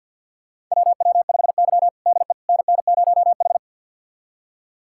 Morse code, heard as a keyed tone, about 52 wpm, 700 Hz. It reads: WW5Y DE NN0H
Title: USA Callsign DE Callsign - 4 Characters - 50wpm